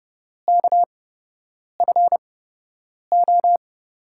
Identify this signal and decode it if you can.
Morse code, heard as a keyed tone, about 31 wpm, spaced as for 8 wpm, 700 Hz. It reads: KFO
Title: USA Call Signs - 6 Characters - 8x Character Spacing - 30wpm